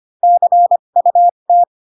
Morse code, heard as a keyed tone, about 25 wpm, spaced as for 22 wpm, 700 Hz. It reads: CUT
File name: Pinocchio / Pinocchio ch4 18 25 ik1hge